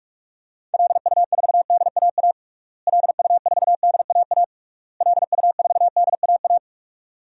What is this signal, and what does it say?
Morse code, heard as a keyed tone, about 41 wpm, 700 Hz. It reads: LU4DAA LU4DAA LU4DAA
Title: DXCC-100 Call Signs - 6 Characters - Repeated 3x - 45wpm